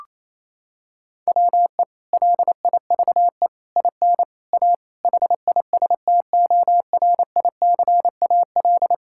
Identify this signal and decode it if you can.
Morse code, heard as a keyed tone, about 28 wpm, 700 Hz. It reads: WE LIVE IN A HISTORICAL
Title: Sentences from Top 1000 Words - Review Part 5 - 28wpm